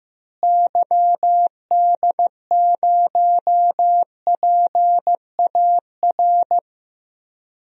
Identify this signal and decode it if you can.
Morse code, heard as a keyed tone, about 15 wpm, 700 Hz. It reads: YD0PAR
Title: DXCC-100 Call Signs - 6 Characters - Repeated 2x - 15wpm